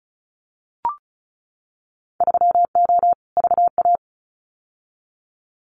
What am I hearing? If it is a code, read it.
Morse code, heard as a keyed tone, about 34 wpm, 700 Hz. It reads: 3O VA
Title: Winter Field Day Exchanges 35wpm